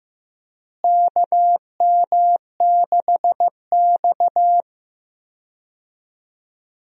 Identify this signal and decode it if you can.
Morse code, heard as a keyed tone, about 15 wpm, 700 Hz. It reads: KM6X
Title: USA Call Signs - 4 Characters - Repeated 3x - 15wpm